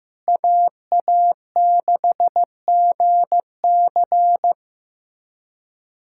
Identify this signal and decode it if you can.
Morse code, heard as a keyed tone, about 16 wpm, 700 Hz. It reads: AA6GC
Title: USA Callsign DE Callsign - 5 Characters - 15wpm